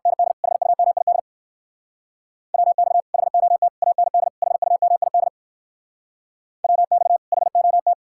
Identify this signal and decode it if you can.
Morse code, heard as a keyed tone, about 57 wpm, 700 Hz. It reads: ND HUMID WX HOT AND HUMID WX HOT